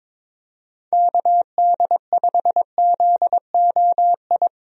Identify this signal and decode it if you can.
Morse code, heard as a keyed tone, about 22 wpm, 700 Hz. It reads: KD5ZOI
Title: USA Call Signs - 6 Characters - Repeated 3x - 22wpm